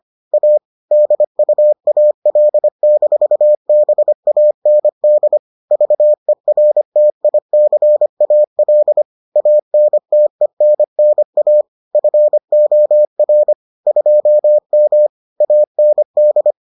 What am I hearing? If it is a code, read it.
Morse code, heard as a keyed tone, about 25 wpm, 600 Hz. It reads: A DUAL-BAND VERTICAL ANTENNA FOR 2M AND